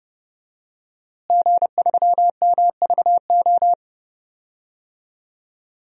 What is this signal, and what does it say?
Morse code, heard as a keyed tone, about 30 wpm, 700 Hz. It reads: G3MVO
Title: DXCC-100 Call Signs - 5 Characters - Repeated 2x - 30wpm